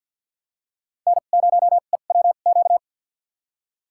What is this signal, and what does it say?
Morse code, heard as a keyed tone, about 53 wpm, spaced as for 37 wpm, 700 Hz. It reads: N0EWQ
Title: USA Call Signs - 5 Characters - 2x Character Spacing - 50wpm